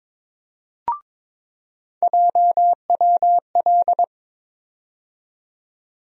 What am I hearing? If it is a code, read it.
Morse code, heard as a keyed tone, about 22 wpm, 700 Hz. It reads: JWL
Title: USA Call Sign Postfixes - 3 Letters 22wpm